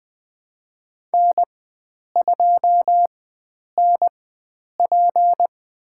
Morse code, heard as a keyed tone, about 20 wpm, spaced as for 9 wpm, 700 Hz. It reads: N2NP
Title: USA Call Signs - 4x Character Spacing - 20wpm